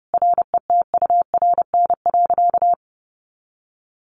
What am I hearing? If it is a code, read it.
Morse code, heard as a keyed tone, about 30 wpm, 700 Hz. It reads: RETURN.
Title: Sentences from Top 1100 Words 30wpm